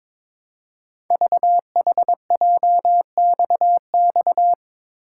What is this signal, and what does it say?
Morse code, heard as a keyed tone, about 22 wpm, 700 Hz. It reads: VHJXX
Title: Code Groups - 5 Letters - 22wpm